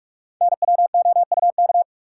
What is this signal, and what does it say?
Morse code, heard as a keyed tone, about 44 wpm, 700 Hz. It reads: NWOUK
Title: Code Groups - 5 Letters - 45wpm